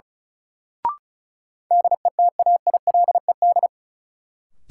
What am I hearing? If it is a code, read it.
Morse code, heard as a keyed tone, about 36 wpm, 700 Hz. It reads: DETAILED
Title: Sentences from Top 1500 Words - New Word Drill Alt - 35wpm